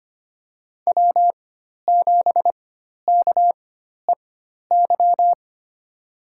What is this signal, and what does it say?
Morse code, heard as a keyed tone, about 25 wpm, spaced as for 12 wpm, 700 Hz. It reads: W7KEY
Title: USA Call Signs - 4x Character Spacing - 25wpm